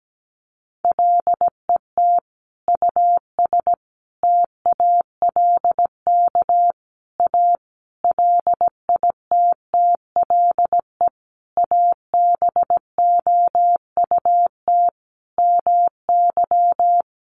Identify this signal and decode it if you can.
Morse code, heard as a keyed tone, about 17 wpm, 700 Hz. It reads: LET US TALK A LITTLE ABOUT MY
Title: Sentences from Top 400 Words 17wpm